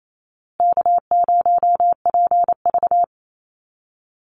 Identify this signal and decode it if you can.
Morse code, heard as a keyed tone, about 28 wpm, 700 Hz. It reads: K0PV
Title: Call Signs Repeated 2x - 28wpm